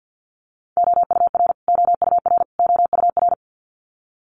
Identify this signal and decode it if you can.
Morse code, heard as a keyed tone, about 48 wpm, 700 Hz. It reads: OUR OUR OUR
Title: Top 500 Words Repeated 50wpm